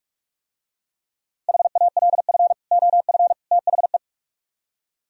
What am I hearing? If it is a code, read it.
Morse code, heard as a keyed tone, about 44 wpm, 700 Hz. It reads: HALF OF THE